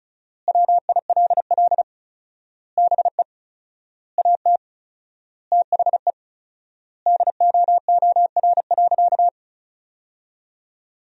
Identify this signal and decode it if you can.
Morse code, heard as a keyed tone, about 36 wpm, 700 Hz. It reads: WILL BE AT THE DOOR.
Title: 4x Word-Spacing - Sentences from Top 400 Words - 35wpm